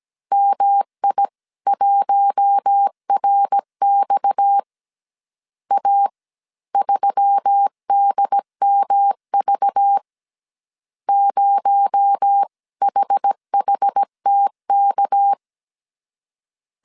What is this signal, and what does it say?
Morse code, heard as a keyed tone, about 17 wpm, spaced as for 15 wpm, 800 Hz. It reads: MI1RX A3DMV 0HHTK
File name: CW / DET / 1 6 1500C S A 16wpm 2635